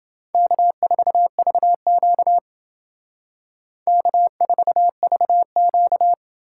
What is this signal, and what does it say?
Morse code, heard as a keyed tone, about 29 wpm, 700 Hz. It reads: K4VQ K4VQ